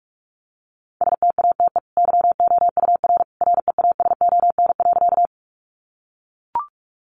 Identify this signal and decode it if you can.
Morse code, heard as a keyed tone, about 44 wpm, 700 Hz. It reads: STATE YOUR REASON.
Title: Sentences from Top 700 Words 45wpm